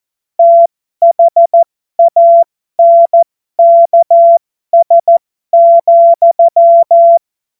Morse code, heard as a keyed tone, about 14 wpm, spaced as for 12 wpm, 680 Hz. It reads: THANKS,
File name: Morsekurs / Witze